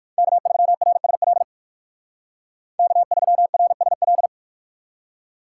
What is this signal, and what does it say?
Morse code, heard as a keyed tone, about 62 wpm, 700 Hz. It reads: K3RSL K3RSL